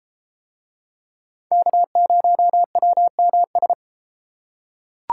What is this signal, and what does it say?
Morse code, heard as a keyed tone, about 32 wpm, 700 Hz. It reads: K0WMS